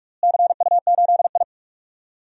Morse code, heard as a keyed tone, about 47 wpm, 700 Hz. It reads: CU8I
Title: DXCC-100 Call Signs - 4 Characters - Repeated 3x - 45wpm